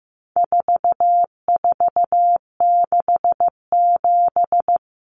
Morse code, heard as a keyed tone, about 15 wpm, 700 Hz. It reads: 4467
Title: Numbers - 4-Digit 15wpm